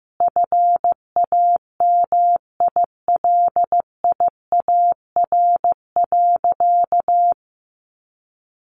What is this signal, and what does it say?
Morse code, heard as a keyed tone, about 15 wpm, 700 Hz. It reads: FAMILIAR.
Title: Sentences from Top 1000 Words 15wpm